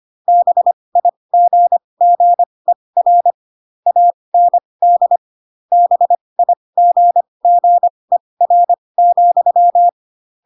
Morse code, heard as a keyed tone, about 25 wpm, spaced as for 20 wpm, 700 Hz. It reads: BIGGER AND BIGGER,